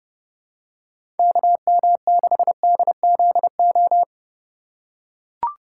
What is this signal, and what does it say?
Morse code, heard as a keyed tone, about 30 wpm, 700 Hz. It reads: KM6DZO
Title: USA Call Signs - 3x Character Spacing - 30wpm